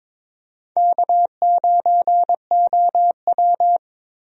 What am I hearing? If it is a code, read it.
Morse code, heard as a keyed tone, about 22 wpm, 700 Hz. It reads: K9OW
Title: USA Call Signs - 4 Characters - Repeated 3x - 22wpm